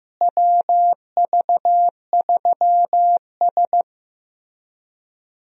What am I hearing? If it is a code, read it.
Morse code, heard as a keyed tone, about 15 wpm, 700 Hz. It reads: WV3S
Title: USA Call Signs - 4 Characters - Repeated 3x - 15wpm